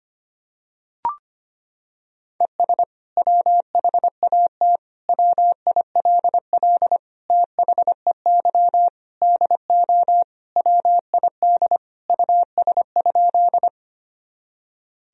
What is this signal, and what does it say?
Morse code, heard as a keyed tone, about 25 wpm, 700 Hz. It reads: ES WHAT WILL THEY DO WID US?
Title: Sentences from Top 200 Words and Abbreviations 25wpm